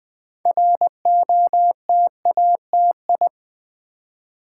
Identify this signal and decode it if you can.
Morse code, heard as a keyed tone, about 20 wpm, 700 Hz. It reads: ROTATI
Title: Build A Word - 2 Letters At A Time - 8-Letter Words - 20wpm